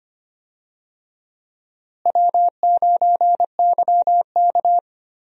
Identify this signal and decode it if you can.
Morse code, heard as a keyed tone, about 25 wpm, 700 Hz. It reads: W9YK
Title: USA Call Signs - 4 Characters - Repeated 3x - 25wpm